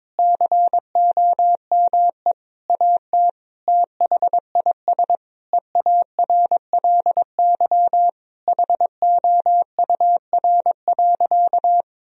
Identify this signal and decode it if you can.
Morse code, heard as a keyed tone, about 22 wpm, 700 Hz. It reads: COME AT THIS EARLY HOUR.
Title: Sentences from Top 1000 Words - Review Part 2 - 22wpm